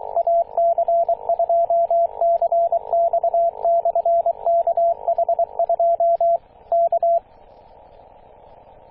Morse code, heard as a keyed tone, about 23 wpm, 680 Hz, with radio noise. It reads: AC2CX/KH2 K
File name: posts / 2016-june-26-0520z-ac2cx-kh2-21-027mhz-cw